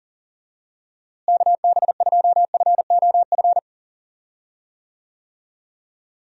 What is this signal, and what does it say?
Morse code, heard as a keyed tone, about 40 wpm, 700 Hz. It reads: KB2FOF
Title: USA Call Signs - 6 Characters - Repeated 2x - 40wpm